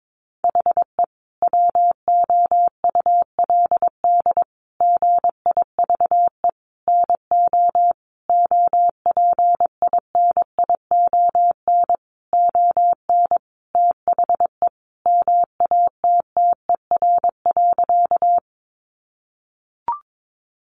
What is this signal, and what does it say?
Morse code, heard as a keyed tone, about 22 wpm, 700 Hz. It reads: HE WOULD GIVE NO OPINION ON THE MATTER.
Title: Sentences from Top 1200 Words 22wpm